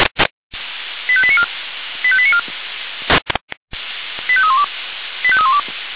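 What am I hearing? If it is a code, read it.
Five-tone selective call (ZVEI), heard as quick runs of five tones, tones of about 70 ms. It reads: 85894 85894 85312 85312
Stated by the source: ZVEI five-tone selective call received on an SDR.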